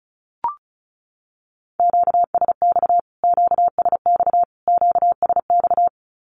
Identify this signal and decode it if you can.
Morse code, heard as a keyed tone, about 34 wpm, 700 Hz. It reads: QSX QSX QSX
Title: Q-Codes Repeated 3x 35wpm